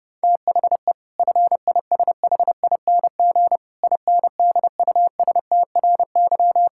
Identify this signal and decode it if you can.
Morse code, heard as a keyed tone, about 30 wpm, 700 Hz. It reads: THE FISHING INDUSTRY
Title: Sentences from Top 1000 Words - Review Part 6 - 30wpm